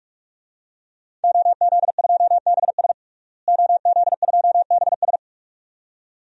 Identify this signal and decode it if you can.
Morse code, heard as a keyed tone, about 46 wpm, 700 Hz. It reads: OZ2BS OZ2BS
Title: DXCC-100 Call Signs - 5 Characters - Repeated 2x - 45wpm